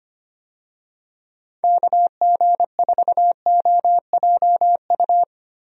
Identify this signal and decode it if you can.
Morse code, heard as a keyed tone, about 25 wpm, 700 Hz. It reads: KG4OJU
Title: USA Call Signs - 6 Characters - 25wpm